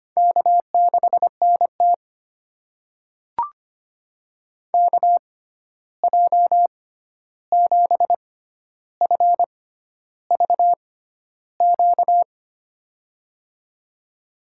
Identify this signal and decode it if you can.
Morse code, heard as a keyed tone, about 25 wpm, spaced as for 11 wpm, 700 Hz. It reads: K6NT KJ7FVQ
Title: USA Call Signs - 6x Character Spacing - 25wpm